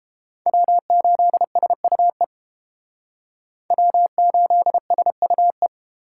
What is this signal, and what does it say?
Morse code, heard as a keyed tone, about 30 wpm, 700 Hz. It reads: W8SUE W8SUE